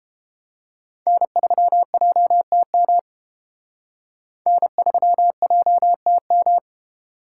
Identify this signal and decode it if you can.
Morse code, heard as a keyed tone, about 29 wpm, 700 Hz. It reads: N3JTM N3JTM